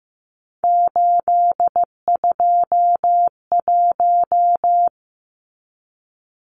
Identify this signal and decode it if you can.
Morse code, heard as a keyed tone, about 15 wpm, 700 Hz. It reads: 821
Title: Numbers - 3-Digit 15wpm